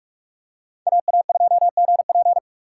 Morse code, heard as a keyed tone, about 43 wpm, 690 Hz. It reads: AA2GP
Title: USA Call Signs - 5 Characters - 4x Character Spacing - 45wpm